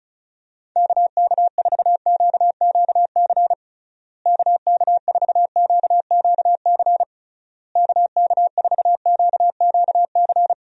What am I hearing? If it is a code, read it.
Morse code, heard as a keyed tone, about 34 wpm, 700 Hz. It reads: KK4QQC KK4QQC KK4QQC
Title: USA Call Signs - 6 Characters - Repeated 3x - 35wpm